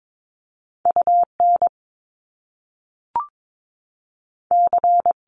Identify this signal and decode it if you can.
Morse code, heard as a keyed tone, about 22 wpm, 700 Hz. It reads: UN C
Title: Top Two Letter Combinations in Words 22wpm